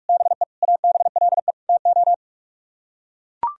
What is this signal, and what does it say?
Morse code, heard as a keyed tone, about 44 wpm, 700 Hz. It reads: BE ABLE TO